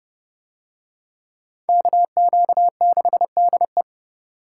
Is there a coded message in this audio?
Yes: Morse code from 1.7 to 3.8 s, heard as a keyed tone, about 28 wpm, 700 Hz. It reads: KQ6DE